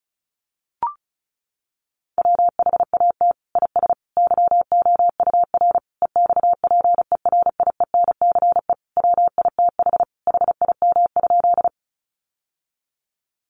Morse code, heard as a keyed tone, about 34 wpm, 700 Hz. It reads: WHAT IS YOUR EXPERIENCE WITH HIM?
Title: Sentences from Top 600 Words 35wpm